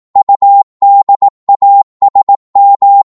Morse code, heard as a keyed tone, about 18 wpm, 800 Hz. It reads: UDASM
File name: Morse / ltr-18wpm